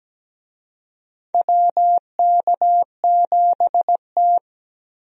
Morse code, heard as a keyed tone, about 17 wpm, 700 Hz. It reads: WK7T
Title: USA Call Signs - 4 Characters - Repeated 2x - 17wpm